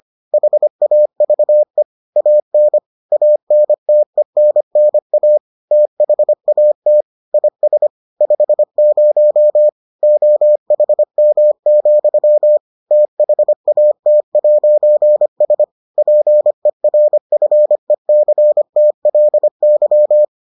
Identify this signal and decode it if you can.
Morse code, heard as a keyed tone, about 25 wpm, 600 Hz. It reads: HAVE AN ANTENNA THAT IS 50 OHM, THAT'S PERFECTLY